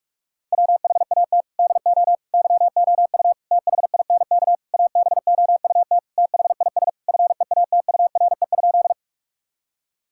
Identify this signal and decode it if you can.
Morse code, heard as a keyed tone, about 41 wpm, 700 Hz. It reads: WHAT DO YOU THINK ABOUT THIS FEATURE?
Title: Sentences from Top 1000 Words - Review Part 5 - 45wpm